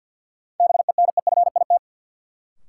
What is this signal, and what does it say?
Morse code, heard as a keyed tone, about 53 wpm, 700 Hz. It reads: BENEFIT
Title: Sentences from Top 1300 Words - New Word Drill Alt - 50wpm